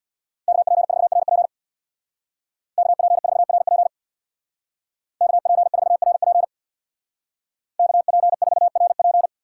Morse code, heard as a keyed tone, about 61 wpm, 690 Hz. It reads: KP4RP KP4RP KP4RP KP4RP
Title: Speed Racing - USA 2x2 Call Signs 50wpm